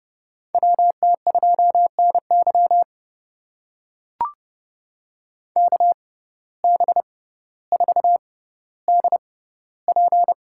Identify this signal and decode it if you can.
Morse code, heard as a keyed tone, about 30 wpm, spaced as for 15 wpm, 700 Hz. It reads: WT2NY KB4DP
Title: USA Call Signs - 6x Character Spacing - 30wpm